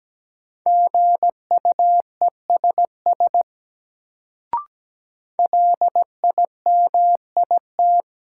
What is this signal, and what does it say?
Morse code, heard as a keyed tone, about 17 wpm, 700 Hz. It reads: GUESS LIMIT
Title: Sentences from Top 1500 Words - New Word Drill Alt - 17wpm